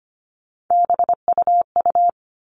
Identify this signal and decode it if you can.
Morse code, heard as a keyed tone, about 25 wpm, 700 Hz. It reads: BUU
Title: All Cut Numbers - 3-Digit 25wpm